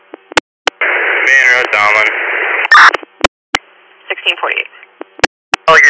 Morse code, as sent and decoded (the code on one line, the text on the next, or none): none